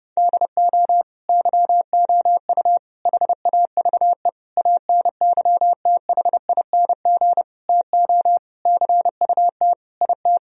Morse code, heard as a keyed tone, about 32 wpm, 700 Hz. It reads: DO YOU HAVE ANYTHING TO CUT IT
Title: Sentences from Top 1000 Words - Review Part 1 - 30wpm